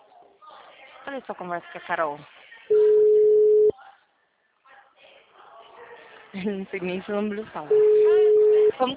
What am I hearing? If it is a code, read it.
Ringback tone (Europe, 425 Hz), with speech.